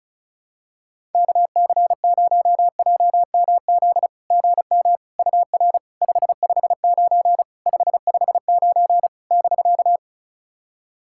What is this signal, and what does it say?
Morse code, heard as a keyed tone, about 36 wpm, 690 Hz. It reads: KC0JMZ GM UR 559 559 <BK>
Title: SOTA QSO Practice 35wpm